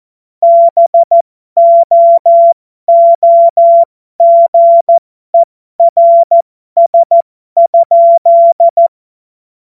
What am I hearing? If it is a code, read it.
Morse code, heard as a keyed tone, about 14 wpm, spaced as for 12 wpm, 680 Hz. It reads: BOOGERS?
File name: Morsekurs / Witze